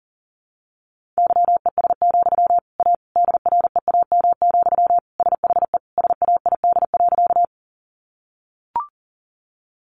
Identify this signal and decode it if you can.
Morse code, heard as a keyed tone, about 40 wpm, 700 Hz. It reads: YES, A DREAM, SHE SAID.
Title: Sentences from Top 1500 Words 40wpm